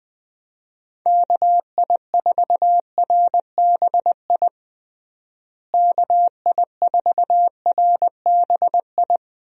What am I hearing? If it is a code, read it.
Morse code, heard as a keyed tone, about 21 wpm, 700 Hz. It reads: KI4RBI KI4RBI
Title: USA Call Signs - 6 Characters - Repeated 2x - 20wpm